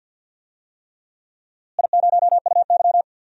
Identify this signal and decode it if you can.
Morse code, heard as a keyed tone, about 47 wpm, 700 Hz. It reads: I0UY
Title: DXCC-100 Call Signs - 4 Characters - Repeated 3x - 50wpm